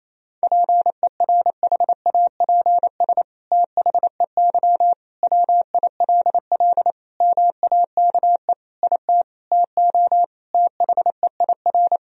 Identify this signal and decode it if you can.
Morse code, heard as a keyed tone, about 28 wpm, 700 Hz. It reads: PERHAPS THEY WILL MAKE IT TO THEIR
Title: Sentences from Top 1000 Words - Review Part 2 - 28wpm